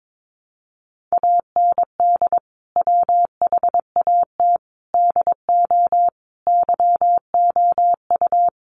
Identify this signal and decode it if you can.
Morse code, heard as a keyed tone, about 22 wpm, 700 Hz. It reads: AND WHAT DO YOU
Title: Sentences from Top 100 Words 22wpm